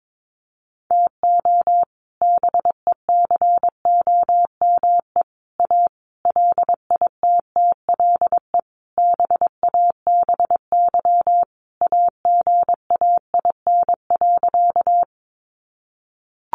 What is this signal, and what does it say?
Morse code, heard as a keyed tone, about 22 wpm, 700 Hz. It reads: TO BECOME A LITTLE BABY AGAIN.
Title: Sentences from Top 1300 Words 22wpm